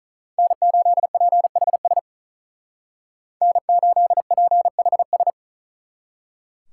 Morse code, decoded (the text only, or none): N8PHS N8PHS